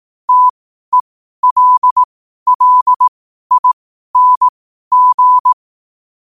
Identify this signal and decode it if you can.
Morse code, heard as a keyed tone, about 18 wpm, spaced as for 12 wpm, 1000 Hz. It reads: TELLING
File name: qotdfiles / 12wpm